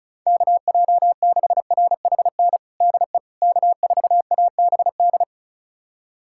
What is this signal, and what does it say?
Morse code, heard as a keyed tone, about 36 wpm, 690 Hz. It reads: KJ6RHN DE K4ABD
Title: USA Callsign DE Callsign 35wpm